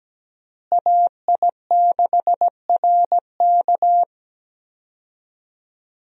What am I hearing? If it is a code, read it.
Morse code, heard as a keyed tone, about 17 wpm, 700 Hz. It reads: AI6RK
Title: USA Call Signs - 5 Characters - Repeated 3x - 17wpm